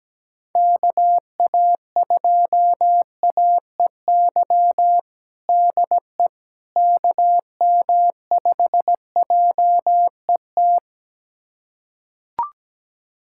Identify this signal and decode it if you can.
Morse code, heard as a keyed tone, about 17 wpm, 700 Hz. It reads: KA2AEY DE KM5JET
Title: USA Callsign DE Callsign - 6 Characters - 17wpm